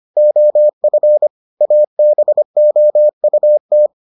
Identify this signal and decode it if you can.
Morse code, heard as a keyed tone, about 25 wpm, 600 Hz. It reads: OF ABOUT